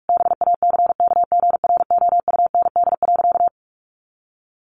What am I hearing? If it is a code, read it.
Morse code, heard as a keyed tone, about 44 wpm, 700 Hz. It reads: BACKGROUND.
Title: Sentences from Top 1500 Words 45wpm